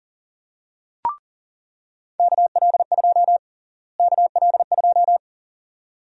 Morse code, heard as a keyed tone, about 40 wpm, 700 Hz. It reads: KL2 KL2
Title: USA Call Sign Prefixes - Repeated 2x - 40wpm